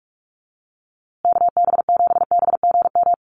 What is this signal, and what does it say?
Morse code, heard as a keyed tone, about 44 wpm, 690 Hz. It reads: KB7BGM
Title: USA Call Signs Repeated 3x 45wpm